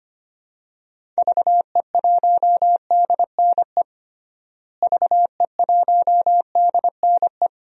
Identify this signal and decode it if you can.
Morse code, heard as a keyed tone, about 25 wpm, 700 Hz. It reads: VE1DNE VE1DNE
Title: DXCC-100 Call Signs - 6 Characters - Repeated 2x - 25wpm